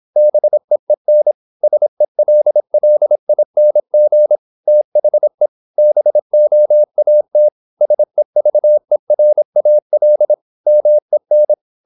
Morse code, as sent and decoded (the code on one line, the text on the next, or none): -... . -. / ... .-.. .-.. .. -. --. / - .... / -... --- .- - / ... ...- . .-. .- .-.. / -- . -.
BEEN SELLING THE BOAT SEVERAL MEN